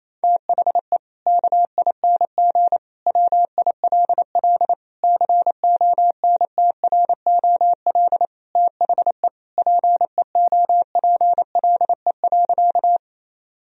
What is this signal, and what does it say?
Morse code, heard as a keyed tone, about 28 wpm, 700 Hz. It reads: THE KING WILL CONTROL THE PEOPLE.
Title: Sentences from Top 1000 Words - Review Part 3 - 28wpm